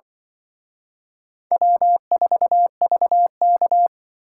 Morse code, heard as a keyed tone, about 23 wpm, 700 Hz. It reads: W4VK
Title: Speed Racing - USA 1x2 Call Signs 20wpm